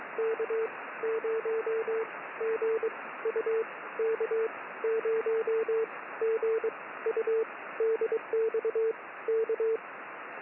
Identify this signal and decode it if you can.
Morse code, heard as a keyed tone, about 22 wpm, spaced as for 16 wpm, 450 Hz, with radio noise. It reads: K0GU K0GU DX K
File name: K0GU